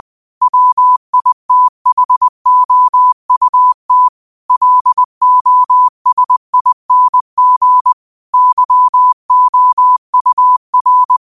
Morse code, heard as a keyed tone, about 20 wpm, 1000 Hz. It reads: WITHOUT LOSING YOUR
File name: qotdfiles / 20wpm